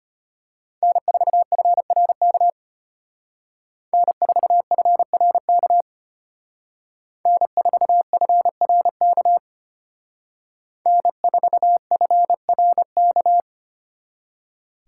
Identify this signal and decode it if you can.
Morse code, heard as a keyed tone, about 32 wpm, 700 Hz. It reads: N4FRK N4FRK N4FRK N4FRK